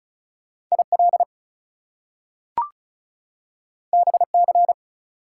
Morse code, heard as a keyed tone, about 36 wpm, 700 Hz. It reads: IL BC
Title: US States and Canadian Provinces 35wpm